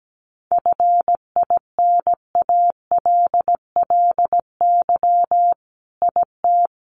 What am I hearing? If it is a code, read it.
Morse code, heard as a keyed tone, about 17 wpm, 700 Hz. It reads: FINALLY IT
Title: Sentences from Top 1000 Words 17wpm